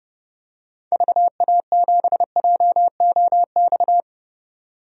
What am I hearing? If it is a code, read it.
Morse code, heard as a keyed tone, about 31 wpm, 700 Hz. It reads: VA7JOX
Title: DXCC-100 Call Signs - 6 Characters - 30wpm